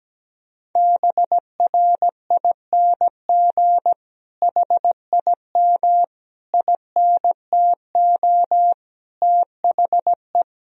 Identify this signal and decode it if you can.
Morse code, heard as a keyed tone, about 17 wpm, 700 Hz. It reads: BRING HIM INTO THE
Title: Sentences from Top 1000 Words - Review Part 1 - 17wpm